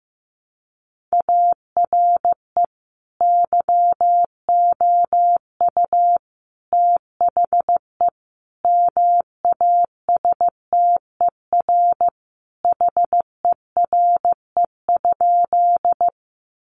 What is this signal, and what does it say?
Morse code, heard as a keyed tone, about 15 wpm, 700 Hz. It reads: ARE YOU THE MASTER HERE?
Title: Sentences from Top 1300 Words 15wpm